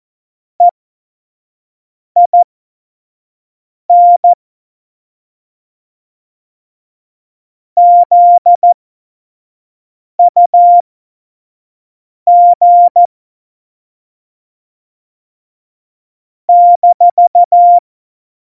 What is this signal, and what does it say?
Morse code, heard as a keyed tone, about 14 wpm, spaced as for 5 wpm, 700 Hz. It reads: EIN ZUG -
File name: Morsekurs / Witze / Witz 13. November 2018, 25 F70